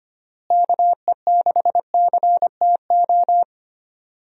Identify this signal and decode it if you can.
Morse code, heard as a keyed tone, about 25 wpm, 700 Hz. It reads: KE6CTO
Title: USA Call Signs - 6 Characters - Repeated 2x - 25wpm